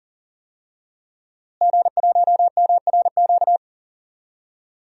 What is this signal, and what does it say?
Morse code, heard as a keyed tone, about 40 wpm, 700 Hz. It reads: G1MRQ
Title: DXCC-100 Call Signs - 5 Characters - Repeated 2x - 40wpm